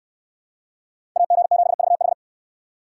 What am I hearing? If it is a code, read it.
Morse code, heard as a keyed tone, about 69 wpm, 700 Hz. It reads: AG7VH